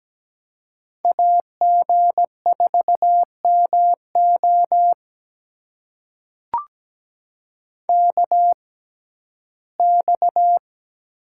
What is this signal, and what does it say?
Morse code, heard as a keyed tone, about 17 wpm, spaced as for 12 wpm, 700 Hz. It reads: AG4MO KX
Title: USA Call Signs - 5 Characters - 6x Character Spacing - 17wpm